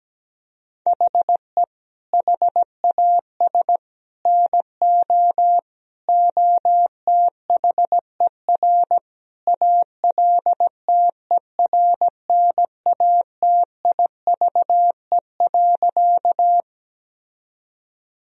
Morse code, heard as a keyed tone, about 17 wpm, 700 Hz. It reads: HE HAS NO OTHER ALTERNATIVE.
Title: Sentences from Top 1000 Words - Review Part 4 - 17wpm